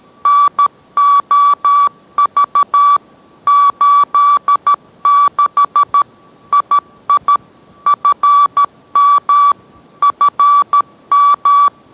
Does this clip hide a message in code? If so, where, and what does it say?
Yes: Morse code from 0.3 to 11.7 s, heard as a keyed tone, about 13 wpm, 1150 Hz. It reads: NOV 86 II FM FM